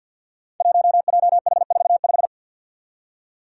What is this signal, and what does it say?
Morse code, heard as a keyed tone, about 53 wpm, 700 Hz. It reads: 1JH45
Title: Five Challenging Letter-Number Combinations 50wpm